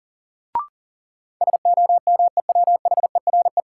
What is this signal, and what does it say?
Morse code, heard as a keyed tone, about 40 wpm, 700 Hz. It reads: SOMEWHERE
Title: Sentences from Top 1400 Words - New Word Drill Alt - 40wpm